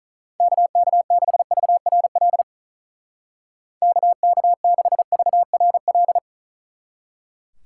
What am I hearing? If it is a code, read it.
Morse code, heard as a keyed tone, about 45 wpm, 700 Hz. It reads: KK6VRL KK6VRL